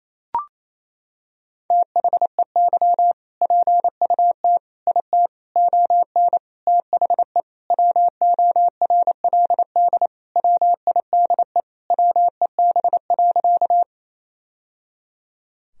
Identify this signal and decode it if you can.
Morse code, heard as a keyed tone, about 28 wpm, 700 Hz. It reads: THEY PUT IT ON THE WORLD WIDE WEB.
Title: Sentences from Top 1000 Words - Review Part 4 - 28wpm